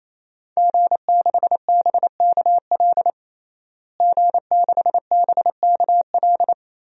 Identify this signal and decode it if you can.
Morse code, heard as a keyed tone, about 28 wpm, 700 Hz. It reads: G6BKL G6BKL
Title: DXCC-100 Call Signs - 5 Characters - Repeated 3x - 28wpm